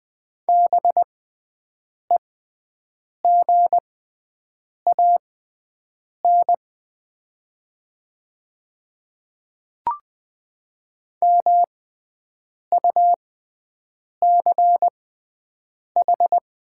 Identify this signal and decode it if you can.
Morse code, heard as a keyed tone, about 20 wpm, spaced as for 7 wpm, 700 Hz. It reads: BEGAN MUCH
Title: 6x Character Spacing - Top 500 Words - 20wpm